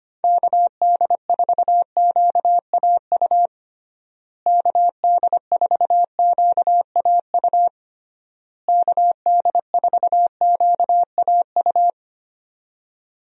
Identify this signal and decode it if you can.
Morse code, heard as a keyed tone, about 25 wpm, 700 Hz. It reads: KD4QAU KD4QAU KD4QAU
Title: USA Call Signs - 6 Characters - Repeated 3x - 25wpm